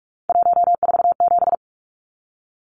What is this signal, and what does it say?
Morse code, heard as a keyed tone, about 44 wpm, 700 Hz. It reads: 147
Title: Numbers - 3-Digit 45wpm